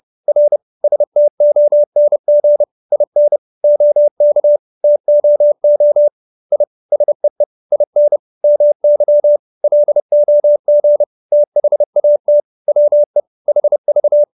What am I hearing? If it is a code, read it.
Morse code, heard as a keyed tone, about 30 wpm, 590 Hz. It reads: R STONG IN OK TOO I SEE IN MY LOG THAT WE HV